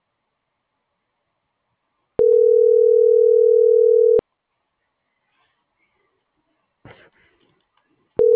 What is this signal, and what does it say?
Ringback tone (North America), 440 + 480 Hz.